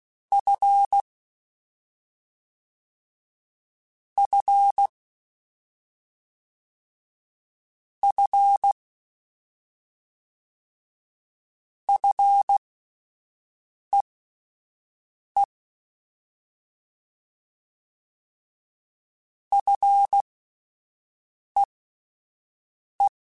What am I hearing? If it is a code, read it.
Morse code, heard as a keyed tone, about 16 wpm, spaced as for 5 wpm, 800 Hz. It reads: F F F FEE FEE
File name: Fs / Single F mp3s / Single F Prefixes 3x2 Twice @5